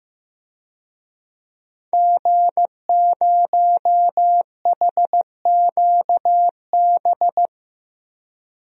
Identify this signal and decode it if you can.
Morse code, heard as a keyed tone, about 14 wpm, 700 Hz. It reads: G0HQB
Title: DXCC-100 Call Signs - 5 Characters - Repeated 3x - 15wpm